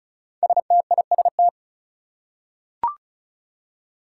Morse code, heard as a keyed tone, about 36 wpm, 700 Hz. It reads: STIST
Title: Five Challenging Letter-Number Combinations 35wpm